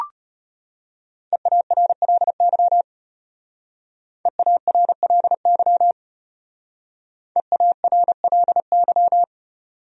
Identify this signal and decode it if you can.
Morse code, heard as a keyed tone, about 33 wpm, 700 Hz. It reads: EARLY EARLY EARLY